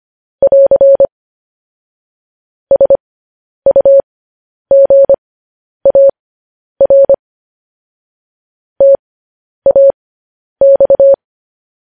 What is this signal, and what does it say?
Morse code, heard as a keyed tone, about 25 wpm, spaced as for 10 wpm, 550 Hz. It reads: <AR> SUGAR TAX
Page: news headlines in morse code at 10 wpm